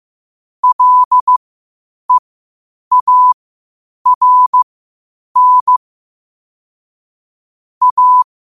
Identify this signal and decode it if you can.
Morse code, heard as a keyed tone, about 15 wpm, spaced as for 8 wpm, 1000 Hz. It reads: LEARN A
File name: qotdfiles / 8wpm